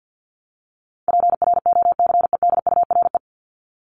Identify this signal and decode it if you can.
Morse code, heard as a keyed tone, about 50 wpm, 700 Hz. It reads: PROCEDURE
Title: Top 1751-2000 Words 50wpm